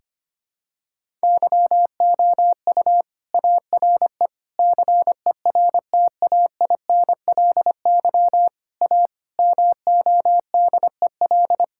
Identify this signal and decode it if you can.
Morse code, heard as a keyed tone, about 25 wpm, 700 Hz. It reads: YOU ARE CERTAINLY A MODEL
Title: Sentences from Top 1000 Words - Review Part 5 - 25wpm